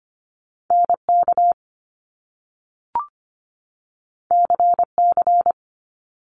Morse code, heard as a keyed tone, about 25 wpm, 700 Hz. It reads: NK CC
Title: Top Two Letter Combinations in Words 25wpm